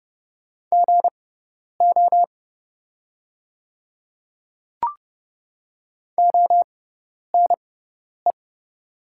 Morse code, heard as a keyed tone, about 30 wpm, spaced as for 10 wpm, 690 Hz. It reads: GO ONE
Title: 6x Character Spacing - Top 100 Words - 30wpm